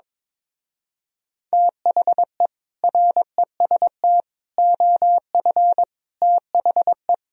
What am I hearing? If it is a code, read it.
Morse code, heard as a keyed tone, about 22 wpm, 700 Hz. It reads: THE REST OF THE
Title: Speed-Racing - Sets of 5 Words 22wpm